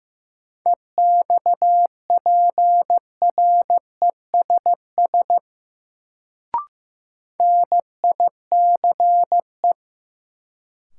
Morse code, heard as a keyed tone, about 15 wpm, 700 Hz. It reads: EXPRESS NICE
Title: Sentences from Top 1100 Words - New Word Drill - 15wpm